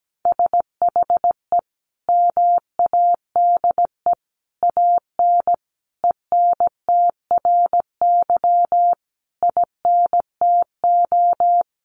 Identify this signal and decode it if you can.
Morse code, heard as a keyed tone, about 17 wpm, 700 Hz. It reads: SHE MADE AN ENTRY INTO
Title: Sentences from Top 1500 Words 17wpm